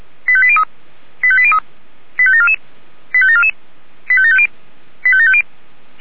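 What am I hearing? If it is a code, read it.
Five-tone selective call (ZVEI), heard as quick runs of five tones, tones of about 70 ms. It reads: 85803 85803 85744 85744 85750 85750
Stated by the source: ZVEI five-tone selective call 85827, received by SDR.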